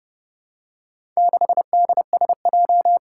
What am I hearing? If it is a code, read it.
Morse code, heard as a keyed tone, about 30 wpm, 700 Hz. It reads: 6DSJ
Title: Four Challenging Letter-Number Combinations 30wpm